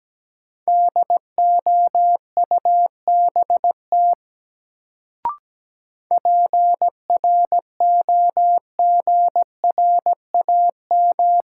Morse code, heard as a keyed tone, about 17 wpm, 700 Hz. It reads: DOUBT PROGRAM
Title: Sentences from Top 1500 Words - New Word Drill Alt - 17wpm